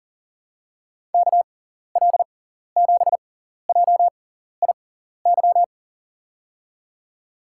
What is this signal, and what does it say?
Morse code, heard as a keyed tone, about 40 wpm, spaced as for 14 wpm, 700 Hz. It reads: KL7JIY
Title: USA Call Signs - 6 Characters - 6x Character Spacing - 40wpm